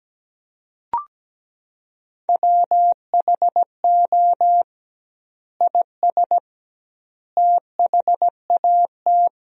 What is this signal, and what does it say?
Morse code, heard as a keyed tone, about 17 wpm, 700 Hz. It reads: WHO IS THAT
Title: 2x Word-Spacing - Sentences from Top 200 Words - 17wpm